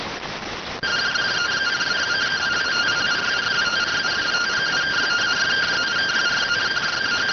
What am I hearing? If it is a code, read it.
Two-tone teleprinter signal, about 170 Hz shift.